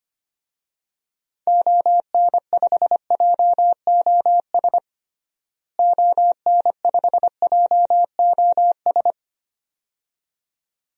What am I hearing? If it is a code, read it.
Morse code, heard as a keyed tone, about 25 wpm, 700 Hz. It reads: ON5JOS ON5JOS
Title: DXCC-100 Call Signs - 6 Characters - Repeated 2x - 25wpm